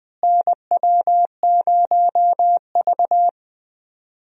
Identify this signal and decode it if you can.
Morse code, heard as a keyed tone, about 20 wpm, 700 Hz. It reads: NW0V
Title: USA Call Signs - 4 Characters - Repeated 2x - 20wpm